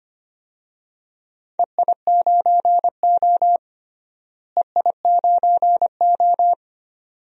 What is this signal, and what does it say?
Morse code, heard as a keyed tone, about 25 wpm, 700 Hz. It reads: EI9O EI9O
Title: DXCC-100 Call Signs - 4 Characters - Repeated 2x - 25wpm